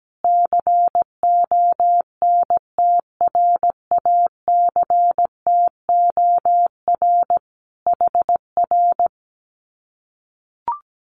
Morse code, heard as a keyed tone, about 17 wpm, 700 Hz. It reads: CONTRACTOR HR
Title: Common QSO Phrases Encore 17wpm